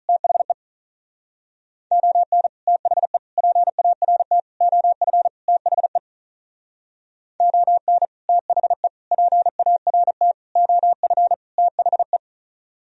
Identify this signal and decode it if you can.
Morse code, heard as a keyed tone, about 39 wpm, 700 Hz. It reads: THE ON THE PART OF THE ON THE PART OF THE
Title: Speed-Racing - Sets of 5 Words 35wpm